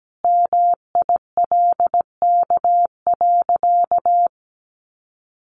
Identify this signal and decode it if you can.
Morse code, heard as a keyed tone, about 17 wpm, 700 Hz. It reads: MILK.
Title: Sentences from Top 1500 Words 17wpm